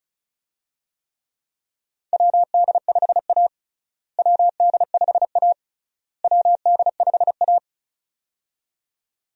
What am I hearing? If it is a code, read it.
Morse code, heard as a keyed tone, about 36 wpm, 700 Hz. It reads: WD5A WD5A WD5A
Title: USA Call Signs - 4 Characters - Repeated 3x - 35wpm